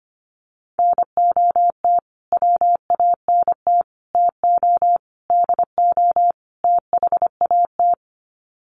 Morse code, heard as a keyed tone, about 25 wpm, 700 Hz. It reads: NOT WANT TO DO THAT
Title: Sets of 5 Words Encore 25wpm